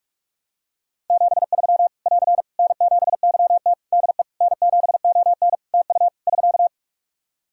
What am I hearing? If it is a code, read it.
Morse code, heard as a keyed tone, about 46 wpm, 700 Hz. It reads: 73 <AR> N7YT DE N7ON TU <SK>
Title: Practice QSO - Exchange 4 - Ending - 45wpm